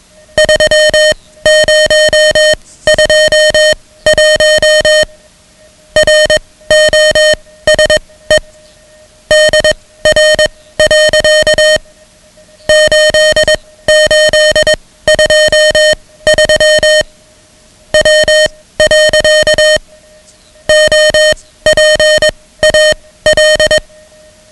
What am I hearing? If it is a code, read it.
Morse code, heard as a keyed tone, about 21 wpm, spaced as for 16 wpm, 620 Hz, with noise. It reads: 3021 ROSE DR. 8823 W. OPAL
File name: ham radio / CodeQuick / Code Quick 10